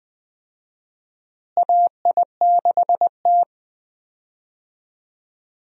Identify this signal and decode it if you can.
Morse code, heard as a keyed tone, about 20 wpm, 700 Hz. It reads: AI6T
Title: USA Call Signs - 4 Characters - Repeated 3x - 20wpm